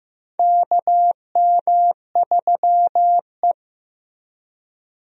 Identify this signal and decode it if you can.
Morse code, heard as a keyed tone, about 15 wpm, 700 Hz. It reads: KM3E
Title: USA Call Signs - 4 Characters - Repeated 3x - 15wpm